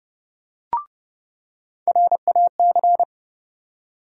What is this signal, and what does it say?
Morse code, heard as a keyed tone, about 30 wpm, 700 Hz. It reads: RAC
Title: USA Call Sign Postfixes - 3 Letters 30wpm